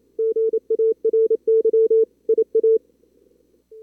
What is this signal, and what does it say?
Morse code, heard as a keyed tone, about 28 wpm, 440 Hz. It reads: GARY IA
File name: CWSprint / Feb2022 / k5tr-feb-na-sprint-cw-k5ot -2300